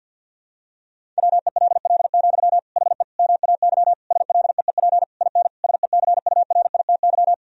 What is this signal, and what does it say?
Morse code, heard as a keyed tone, about 50 wpm, 700 Hz. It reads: WELL, HE MAY SLEEP IN SECURITY
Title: Sentences from Top 1000 Words - Review Part 4 - 50wpm